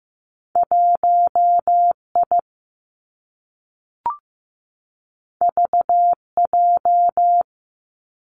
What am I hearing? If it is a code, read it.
Morse code, heard as a keyed tone, about 15 wpm, 700 Hz. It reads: 1 I V J
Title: Two Challenging Letter-Number Combinations 15wpm